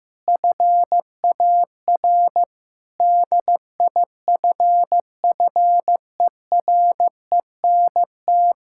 Morse code, heard as a keyed tone, about 16 wpm, 700 Hz. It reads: FAR DIFFERENT
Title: Sentences from Top 1000 Words - Review Part 5 - 15wpm